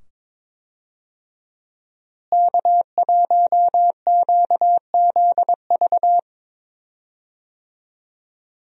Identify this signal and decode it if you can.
Morse code, heard as a keyed tone, about 22 wpm, 700 Hz. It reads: K1QZV
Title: USA Call Signs - 5 Characters - Repeated 2x - 22wpm